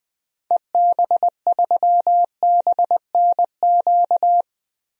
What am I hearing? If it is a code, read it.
Morse code, heard as a keyed tone, about 20 wpm, 700 Hz. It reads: EB3BNQ
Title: DXCC-100 Call Signs - 6 Characters - Repeated 2x - 20wpm